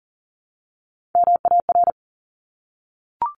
Morse code, heard as a keyed tone, about 40 wpm, 700 Hz. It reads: MAR
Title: ARRL Sections 40wpm